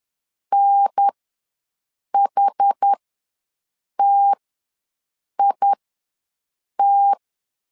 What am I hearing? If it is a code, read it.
Morse code, heard as a keyed tone, about 10 wpm, spaced as for 6 wpm, 800 Hz. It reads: NHTIT
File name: LEZIONI / 4 Q ws 10wpm 2306